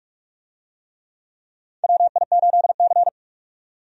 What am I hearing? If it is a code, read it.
Morse code, heard as a keyed tone, about 44 wpm, 700 Hz. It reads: WI8C
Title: USA Call Signs - 4 Characters - 45wpm